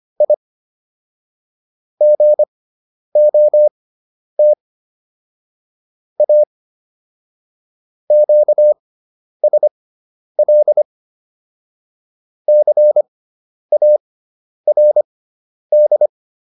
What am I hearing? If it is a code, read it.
Morse code, heard as a keyed tone, about 25 wpm, spaced as for 10 wpm, 600 Hz. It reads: I GOT A QSL CARD